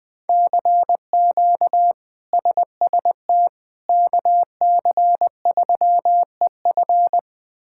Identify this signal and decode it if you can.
Morse code, heard as a keyed tone, about 21 wpm, 700 Hz. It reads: CQ SST KC3EF
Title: K1USN SST QSO Practice - 20wpm